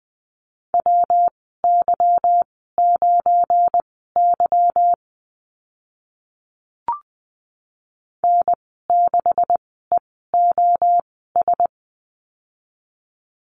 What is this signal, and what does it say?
Morse code, heard as a keyed tone, about 20 wpm, spaced as for 14 wpm, 700 Hz. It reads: WY9Y N6EOS
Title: 207 Lesson 33g - Letter j - Call Signs 20wpm